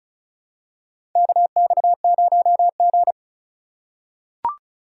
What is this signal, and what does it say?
Morse code, heard as a keyed tone, about 36 wpm, 700 Hz. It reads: KX0G
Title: USA Call Signs - 4 Characters - 2x Character Spacing - 35wpm